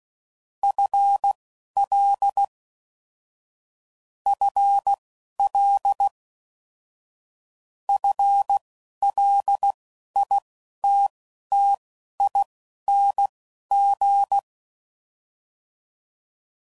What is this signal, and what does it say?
Morse code, heard as a keyed tone, about 16 wpm, spaced as for 11 wpm, 800 Hz. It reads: FL FL FLITTING